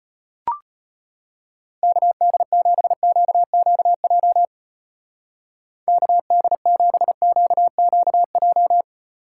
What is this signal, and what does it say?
Morse code, heard as a keyed tone, about 31 wpm, 700 Hz. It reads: KD7QQJ KD7QQJ
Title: Speed Racing - USA Call Signs 25wpm